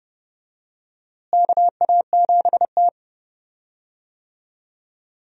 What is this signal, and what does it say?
Morse code, heard as a keyed tone, about 31 wpm, 700 Hz. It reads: KA7T
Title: USA Call Signs - 4 Characters - Repeated 2x - 30wpm